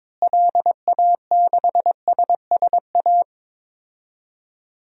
Morse code, heard as a keyed tone, about 22 wpm, 700 Hz. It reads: LA6SSA
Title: DXCC-100 Call Signs - 6 Characters - Repeated 2x - 22wpm